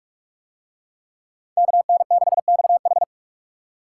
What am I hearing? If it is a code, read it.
Morse code, heard as a keyed tone, about 43 wpm, 700 Hz. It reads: KN6XH